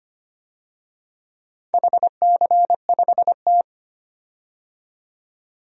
Morse code, heard as a keyed tone, about 25 wpm, 700 Hz. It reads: HC5T
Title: DXCC-100 Call Signs - 4 Characters - 25wpm